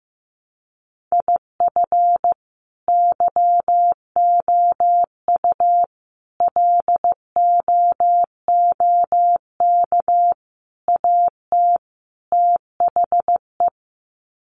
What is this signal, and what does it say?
Morse code, heard as a keyed tone, about 14 wpm, 700 Hz. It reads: IF YOU LOOK AT THE
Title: Sets of 5 Words 15wpm